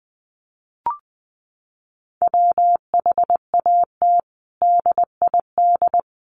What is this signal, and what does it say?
Morse code, heard as a keyed tone, about 20 wpm, 700 Hz. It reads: WHAT DID
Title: Sentences from Top 100 Words 20wpm